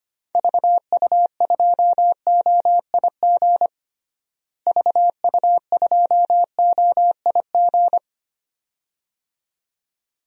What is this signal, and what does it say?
Morse code, heard as a keyed tone, about 25 wpm, 700 Hz. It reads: VU2OIG VU2OIG
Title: DXCC-100 Call Signs - 6 Characters - Repeated 2x - 25wpm